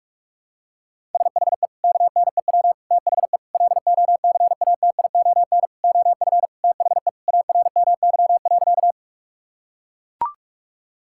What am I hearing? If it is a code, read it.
Morse code, heard as a keyed tone, about 41 wpm, 700 Hz. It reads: SHE KNEW THE LOCATION OF THE ARMY.
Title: Sentences from Top 1000 Words - Review Part 6 - 45wpm